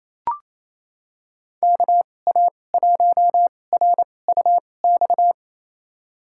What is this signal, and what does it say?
Morse code, heard as a keyed tone, about 28 wpm, spaced as for 20 wpm, 700 Hz. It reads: KA1RUX
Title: USA Call Signs - 2x Character Spacing - 28wpm